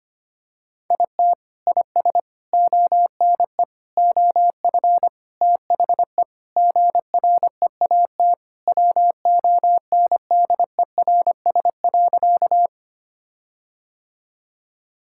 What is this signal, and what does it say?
Morse code, heard as a keyed tone, about 25 wpm, 700 Hz. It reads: IT IS ONE OF THE GREAT WONDERS.
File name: practice / 25wpm / Sentences with 4-Word Phrases - 25wpm